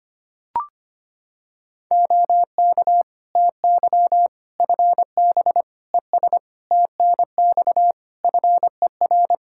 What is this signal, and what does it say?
Morse code, heard as a keyed tone, about 25 wpm, 700 Hz. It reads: OK TY FB ES TNX FER
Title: Practice QSO - Exchange 2 - Info - 25wpm